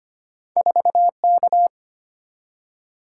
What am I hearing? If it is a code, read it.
Morse code, heard as a keyed tone, about 25 wpm, 700 Hz. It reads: 4K
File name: Two Letter-Number Combinations - 25wpm